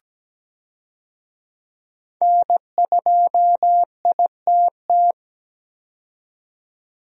Morse code, heard as a keyed tone, about 17 wpm, 700 Hz. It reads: N2ITT
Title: USA Call Signs - 5 Characters - 17wpm